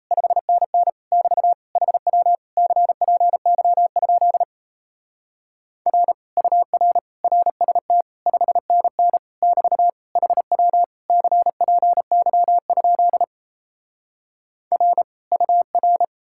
Morse code, heard as a keyed tone, about 38 wpm, 700 Hz. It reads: 5NN <BT> HW CPY? R UR RST 5NN <BT> HW CPY? R UR